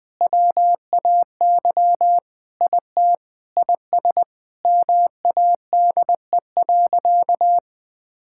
Morse code, heard as a keyed tone, about 20 wpm, 700 Hz. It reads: WAY IT IS MADE.
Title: Sentences with 5-Word Phrases - 20wpm